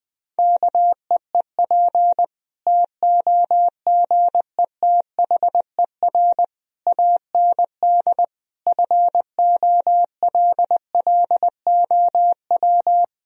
Morse code, heard as a keyed tone, about 20 wpm, 700 Hz. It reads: KEEP TOGETHER AND FOLLOW
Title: Sentences from Top 1000 Words - Review Part 1 - 20wpm